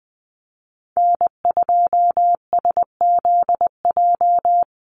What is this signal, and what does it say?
Morse code, heard as a keyed tone, about 20 wpm, 700 Hz. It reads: N2SZJ
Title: USA Call Signs Repeated 2x 20wpm